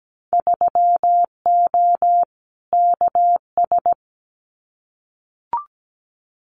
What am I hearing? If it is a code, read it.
Morse code, heard as a keyed tone, about 17 wpm, 700 Hz. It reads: 3O KS
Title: Winter Field Day Exchanges 17wpm